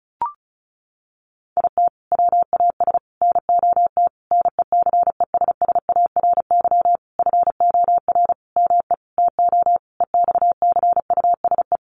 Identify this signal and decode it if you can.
Morse code, heard as a keyed tone, about 34 wpm, 700 Hz. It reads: IT WAS NOT NECESSARY FOR ME TO EXCUSE
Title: Sentences from Top 1300 Words 35wpm